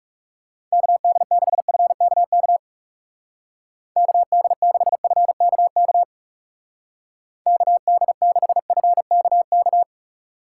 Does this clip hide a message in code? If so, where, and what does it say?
Yes: Morse code from 0.7 to 9.8 s, heard as a keyed tone, about 39 wpm, 700 Hz. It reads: KD6FKK KD6FKK KD6FKK